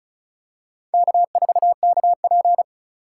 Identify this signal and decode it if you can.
Morse code, heard as a keyed tone, about 36 wpm, 700 Hz. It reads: K4KP
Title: USA Call Signs - 4 Characters - 3x Character Spacing - 35wpm